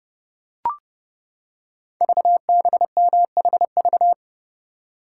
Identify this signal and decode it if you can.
Morse code, heard as a keyed tone, about 30 wpm, 700 Hz. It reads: VBMHV
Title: Code Groups - 5 Letters - 30wpm